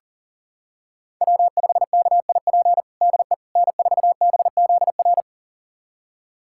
Morse code, heard as a keyed tone, about 40 wpm, 700 Hz. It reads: W5KIP DE N4BZR
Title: USA Callsign DE Callsign - 5 Characters - 40wpm